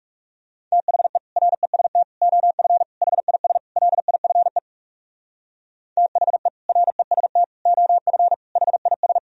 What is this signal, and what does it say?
Morse code, heard as a keyed tone, about 42 wpm, 700 Hz. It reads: THE REST OF HIS LIFE THE REST OF HIS